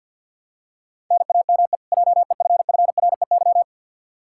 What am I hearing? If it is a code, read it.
Morse code, heard as a keyed tone, about 51 wpm, 700 Hz. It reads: NAME JEFFREY
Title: Practice QSO - Exchange 1 - Beginning - 50wpm